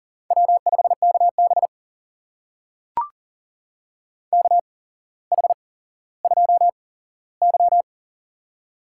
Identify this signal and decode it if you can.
Morse code, heard as a keyed tone, about 40 wpm, spaced as for 17 wpm, 700 Hz. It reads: W5KB KH2Y
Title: USA Call Signs - 4 Characters - 8x Character Spacing - 40wpm